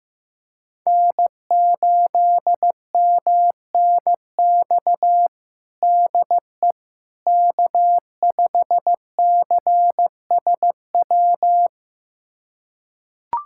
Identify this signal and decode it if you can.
Morse code, heard as a keyed tone, about 15 wpm, 700 Hz. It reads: N8MNX DE K5CSW
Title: USA Callsign DE Callsign 15wpm